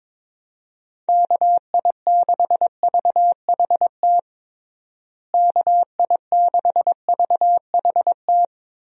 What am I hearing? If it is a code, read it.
Morse code, heard as a keyed tone, about 22 wpm, 700 Hz. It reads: KI6VHT KI6VHT
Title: USA Call Signs - 6 Characters - Repeated 3x - 22wpm